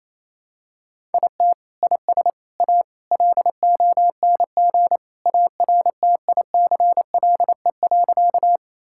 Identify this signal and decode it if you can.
Morse code, heard as a keyed tone, about 27 wpm, 700 Hz. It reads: IT IS A LONG ARTICLE.
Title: Sentences from Top 1000 Words - Review Part 5 - 28wpm